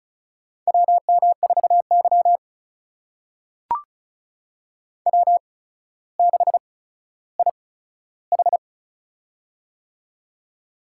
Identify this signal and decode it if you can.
Morse code, heard as a keyed tone, about 36 wpm, spaced as for 15 wpm, 700 Hz. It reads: WM4Y W6IH
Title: USA Call Signs - 8x Character Spacing - 35wpm